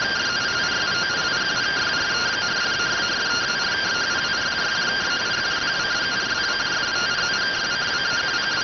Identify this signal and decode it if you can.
Two-tone teleprinter signal, about 170 Hz shift.